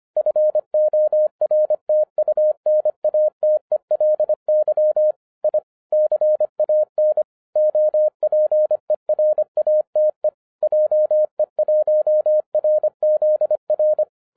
Morse code, heard as a keyed tone, about 24 wpm, 610 Hz. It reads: FORTUNATELY I CAN OPERATE JE1RZR